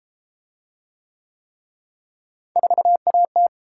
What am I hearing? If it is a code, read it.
Morse code, heard as a keyed tone, about 30 wpm, 700 Hz. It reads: 4AT